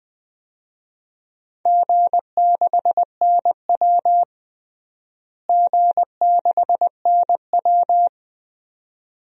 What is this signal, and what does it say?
Morse code, heard as a keyed tone, about 20 wpm, 700 Hz. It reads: G6NW G6NW
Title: DXCC-100 Call Signs - 4 Characters - Repeated 2x - 20wpm